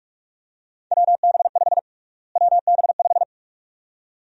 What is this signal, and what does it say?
Morse code, heard as a keyed tone, about 46 wpm, 700 Hz. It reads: WB5 WB5
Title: USA Call Sign Prefixes - Repeated 2x - 45wpm